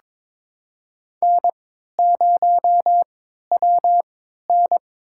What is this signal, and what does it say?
Morse code, heard as a keyed tone, about 23 wpm, spaced as for 13 wpm, 700 Hz. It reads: N0WN
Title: USA Call Signs - 4 Characters - 3x Character Spacing - 22wpm